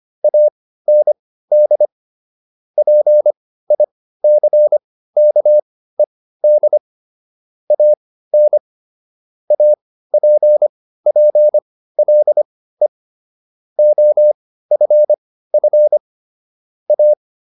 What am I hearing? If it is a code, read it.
Morse code, heard as a keyed tone, about 25 wpm, spaced as for 15 wpm, 600 Hz. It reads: AND PICKED AN APPLE OFF A